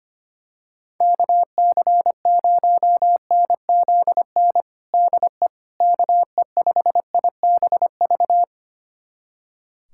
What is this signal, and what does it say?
Morse code, heard as a keyed tone, about 25 wpm, 700 Hz. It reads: KC0NZN DE KE5IBV
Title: USA Callsign DE Callsign 25wpm